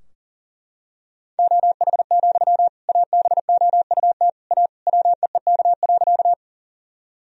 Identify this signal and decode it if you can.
Morse code, heard as a keyed tone, about 40 wpm, 700 Hz. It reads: OH, ABOUT A WEEK.
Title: Sentences from Top 1000 Words - Review Part 2 - 40wpm